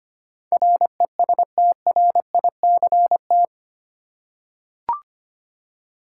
Morse code, heard as a keyed tone, about 25 wpm, 700 Hz. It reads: RESTRICT
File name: 8 Letter Words - First 3 Letters Given - 25wpm